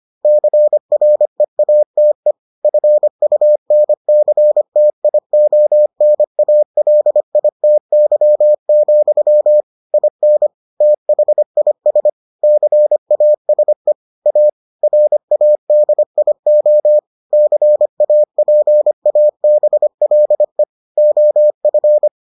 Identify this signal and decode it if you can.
Morse code, heard as a keyed tone, about 25 wpm, 600 Hz. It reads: CREATE FUNCTIONALITY, IN THIS CASE A RADIO CAPABLE OF